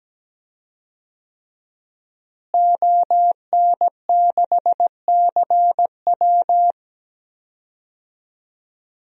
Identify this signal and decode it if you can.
Morse code, heard as a keyed tone, about 17 wpm, 700 Hz. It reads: ON6CW
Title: DXCC-100 Call Signs - 5 Characters - Repeated 3x - 17wpm